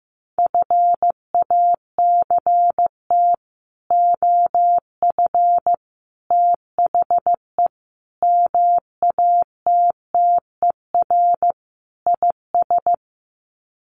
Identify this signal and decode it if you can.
Morse code, heard as a keyed tone, about 15 wpm, 700 Hz. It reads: FACT OF THE MATTER IS
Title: Sets of 5 Words 15wpm